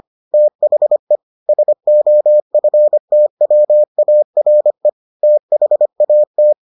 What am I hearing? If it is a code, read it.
Morse code, heard as a keyed tone, about 25 wpm, 600 Hz. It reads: THE SOFTWARE THAT